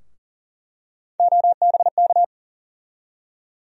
Morse code, heard as a keyed tone, about 40 wpm, 700 Hz. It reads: OBK
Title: USA Call Sign Postfixes - 3 Letters 40wpm